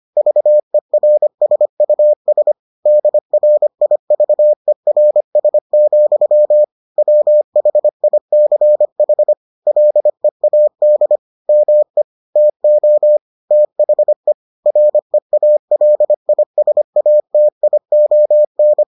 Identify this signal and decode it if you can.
Morse code, heard as a keyed tone, about 25 wpm, 600 Hz. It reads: VERSUS DRIVERS, WHICH LEAD ME TO THE REALISATION